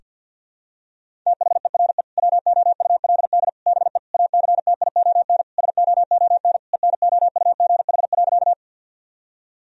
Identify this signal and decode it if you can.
Morse code, heard as a keyed tone, about 50 wpm, 700 Hz. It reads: THERE WOULD BE ACTION SOON ENOUGH.
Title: Sentences from Top 1000 Words - Review Part 4 - 50wpm